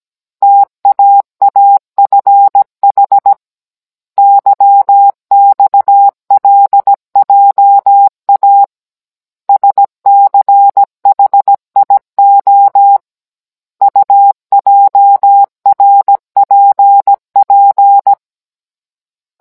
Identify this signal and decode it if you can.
Morse code, heard as a keyed tone, about 17 wpm, 800 Hz. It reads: TAAFH YXLJA SCHIO UJRPP
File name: Cours-NR18-16mm